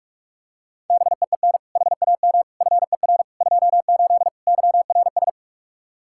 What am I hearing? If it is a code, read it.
Morse code, heard as a keyed tone, about 44 wpm, 700 Hz. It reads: BEEN HAM FER 28 YRS
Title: Practice QSO - Exchange 3 - More Info - 45wpm